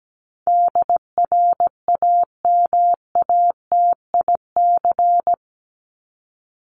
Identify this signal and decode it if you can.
Morse code, heard as a keyed tone, about 17 wpm, 700 Hz. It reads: DRAMATIC
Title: Top 1251-1500 Words 17wpm